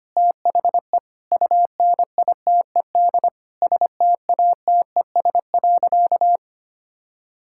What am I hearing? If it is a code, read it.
Morse code, heard as a keyed tone, about 25 wpm, 700 Hz. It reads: THE UNITED STATES.
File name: practice / 25wpm / Sentences with 4-Word Phrases - 25wpm